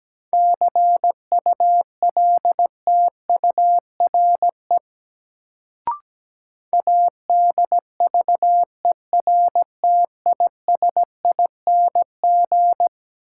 Morse code, heard as a keyed tone, about 17 wpm, 700 Hz. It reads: CULTURE ADVERTISING
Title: Sentences from Top 1000 Words - Review Part 6 - New Word Drill - 17wpm